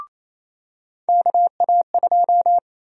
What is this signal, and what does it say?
Morse code, heard as a keyed tone, about 27 wpm, 700 Hz. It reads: KA2
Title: USA Call Sign Prefixes - 28wpm